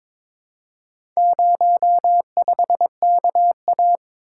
Morse code, heard as a keyed tone, about 22 wpm, 700 Hz. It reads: 05KA
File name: practice / 22wpm / Four Letter-Number Combinations - 22wpm